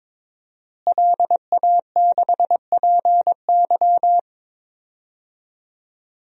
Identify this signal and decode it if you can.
Morse code, heard as a keyed tone, about 22 wpm, 700 Hz. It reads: LA6PY
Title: DXCC-100 Call Signs - 5 Characters - Repeated 2x - 22wpm